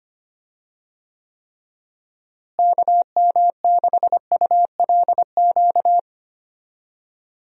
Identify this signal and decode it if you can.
Morse code, heard as a keyed tone, about 25 wpm, 700 Hz. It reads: KM6ULQ
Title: USA Call Signs - 6 Characters - Repeated 3x - 25wpm